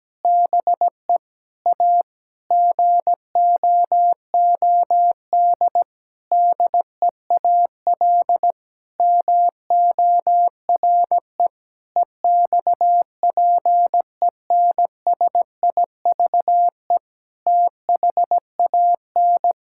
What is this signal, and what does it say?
Morse code, heard as a keyed tone, about 17 wpm, 700 Hz. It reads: BE A GOOD DEAL MORE EXPENSIVE THAN
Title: Sentences from Top 1000 Words - Review Part 6 - 17wpm